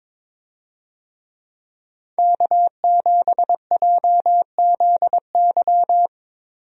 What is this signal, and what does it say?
Morse code, heard as a keyed tone, about 22 wpm, 700 Hz. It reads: K7JZY
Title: USA Call Signs - 5 Characters - 22wpm